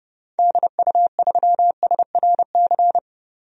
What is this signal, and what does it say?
Morse code, heard as a keyed tone, about 30 wpm, 700 Hz. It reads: DU3SRC
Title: DXCC-100 Call Signs - 6 Characters - 30wpm